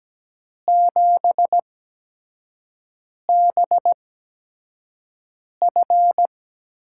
Morse code, heard as a keyed tone, about 17 wpm, spaced as for 5 wpm, 700 Hz. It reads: 7BF
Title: USA Call Signs - 5 Characters - 8x Character Spacing - 17wpm